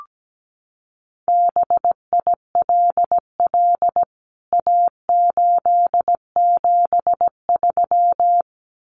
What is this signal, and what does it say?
Morse code, heard as a keyed tone, about 17 wpm, 700 Hz. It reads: BILL A873
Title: CWT Exchange 17wpm